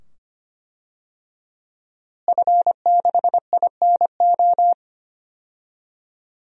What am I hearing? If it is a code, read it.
Morse code, heard as a keyed tone, about 25 wpm, 700 Hz. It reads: F6INO
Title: DXCC-100 Call Signs - 5 Characters - Repeated 2x - 25wpm